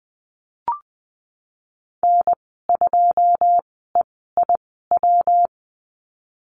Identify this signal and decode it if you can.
Morse code, heard as a keyed tone, about 20 wpm, spaced as for 15 wpm, 700 Hz. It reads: N2EIW
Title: 147 Lesson 24g - Letter y - Call Signs 20wpm